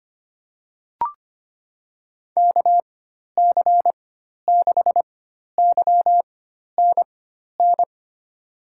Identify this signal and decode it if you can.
Morse code, heard as a keyed tone, about 25 wpm, spaced as for 12 wpm, 700 Hz. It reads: KC6YNN
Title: USA Call Signs - 6 Characters - 4x Character Spacing - 25wpm